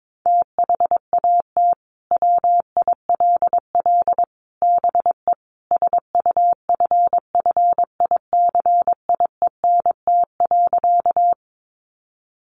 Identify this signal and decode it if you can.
Morse code, heard as a keyed tone, about 22 wpm, 700 Hz. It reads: THAT WILL BE SUFFICIENT.
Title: Sentences from Top 1400 Words 22wpm